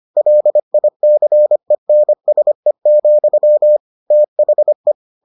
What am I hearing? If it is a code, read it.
Morse code, heard as a keyed tone, about 25 wpm, 600 Hz. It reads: LICENSE, THE